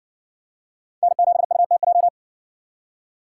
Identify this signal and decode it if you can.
Morse code, heard as a keyed tone, about 58 wpm, 700 Hz. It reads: N7UTJ